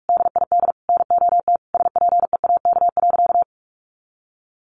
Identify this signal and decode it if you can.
Morse code, heard as a keyed tone, about 44 wpm, 700 Hz. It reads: DID NOT SPEAK.
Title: Sentences from Top 1500 Words 45wpm